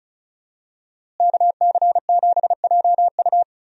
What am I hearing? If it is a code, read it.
Morse code, heard as a keyed tone, about 36 wpm, 700 Hz. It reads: KC7JU
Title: USA Call Signs - 5 Characters - Repeated 2x - 35wpm